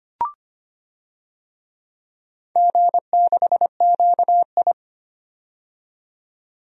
Morse code, heard as a keyed tone, about 25 wpm, 700 Hz. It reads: G6QI
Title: DXCC-100 Call Signs - 4 Characters - 25wpm